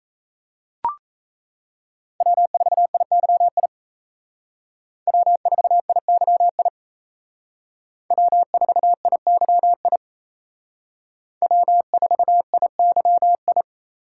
Morse code, heard as a keyed tone, about 36 wpm, 700 Hz. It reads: W4IYI W4IYI W4IYI W4IYI